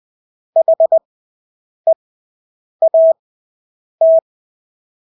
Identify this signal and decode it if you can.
Morse code, heard as a keyed tone, about 20 wpm, spaced as for 8 wpm, 650 Hz. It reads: HEAT
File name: Engelske ord 20/8 wpm